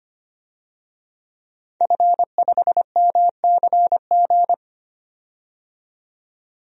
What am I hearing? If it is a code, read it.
Morse code, heard as a keyed tone, about 25 wpm, 700 Hz. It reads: F5MCG
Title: DXCC-100 Call Signs - 5 Characters - 25wpm